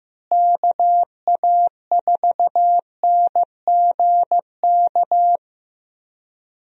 Morse code, heard as a keyed tone, about 15 wpm, 700 Hz. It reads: KA4NGK
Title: USA Call Signs - 6 Characters - Repeated 3x - 15wpm